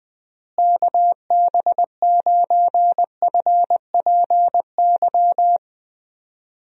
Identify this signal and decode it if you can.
Morse code, heard as a keyed tone, about 20 wpm, 700 Hz. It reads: KB9FPY
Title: USA Call Signs - 6 Characters - Repeated 3x - 20wpm